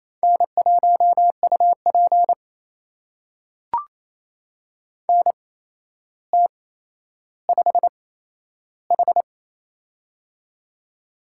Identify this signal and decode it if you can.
Morse code, heard as a keyed tone, about 27 wpm, spaced as for 12 wpm, 700 Hz. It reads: N1UP NT5H
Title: USA Call Signs - 8x Character Spacing - 28wpm